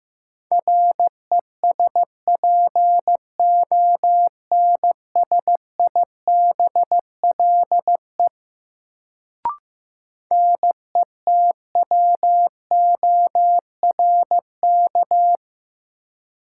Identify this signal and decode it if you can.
Morse code, heard as a keyed tone, about 15 wpm, 700 Hz. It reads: RESPONSIBLE NETWORK
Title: Sentences from Top 1000 Words - Review Part 7 - New Word Drill - 15wpm